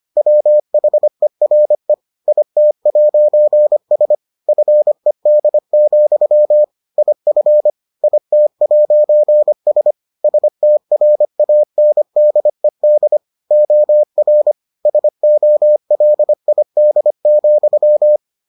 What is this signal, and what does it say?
Morse code, heard as a keyed tone, about 25 wpm, 600 Hz. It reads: WHERE IT'S FED, IF IT'S STRANDED OR SOLID,